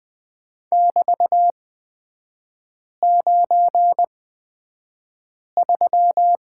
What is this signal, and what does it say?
Morse code, heard as a keyed tone, about 20 wpm, spaced as for 5 wpm, 700 Hz. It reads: <BT>93
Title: 224 Lesson 36e - BT - Three Characters 20wpm